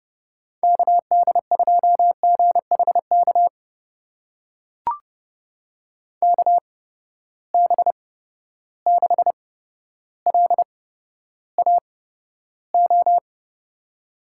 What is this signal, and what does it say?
Morse code, heard as a keyed tone, about 32 wpm, spaced as for 13 wpm, 700 Hz. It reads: KD2GHK KB6LAO
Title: USA Call Signs - 6 Characters - 8x Character Spacing - 30wpm